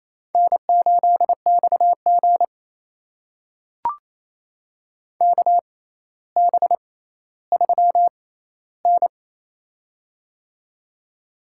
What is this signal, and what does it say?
Morse code, heard as a keyed tone, about 28 wpm, spaced as for 14 wpm, 700 Hz. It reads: N8XG KB3N
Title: USA Call Signs - 4 Characters - 6x Character Spacing - 28wpm